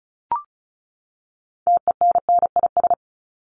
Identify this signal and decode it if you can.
Morse code, heard as a keyed tone, about 34 wpm, 700 Hz. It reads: TENNIS
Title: Top 1501-1750 Words 35wpm